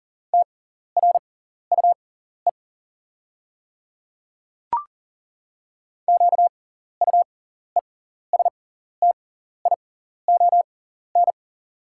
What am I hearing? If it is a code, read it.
Morse code, heard as a keyed tone, about 40 wpm, spaced as for 14 wpm, 690 Hz. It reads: TRUE QUESTION
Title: 6x Character Spacing - Top 500 Words - 40wpm